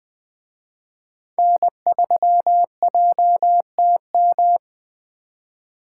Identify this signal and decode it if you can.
Morse code, heard as a keyed tone, about 20 wpm, 700 Hz. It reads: N3JTM
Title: Speed Racing - USA 1x3 Call Signs 17wpm